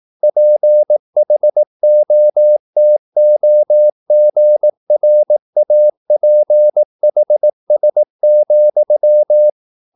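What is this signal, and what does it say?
Morse code, heard as a keyed tone, about 18 wpm, 600 Hz. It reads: PHOTOGRAPHS,
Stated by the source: Morse practice at 18 wpm, English text, from CW med Gnister.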